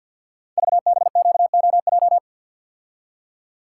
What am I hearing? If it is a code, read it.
Morse code, heard as a keyed tone, about 52 wpm, 700 Hz. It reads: VBQOJ
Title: Code Groups - 5 Letters - 50wpm